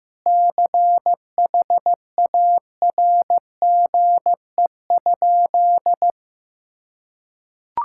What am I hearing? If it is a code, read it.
Morse code, heard as a keyed tone, about 15 wpm, 700 Hz. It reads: CHARGE?
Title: Sentences from Top 1000 Words - Review Part 6 - 15wpm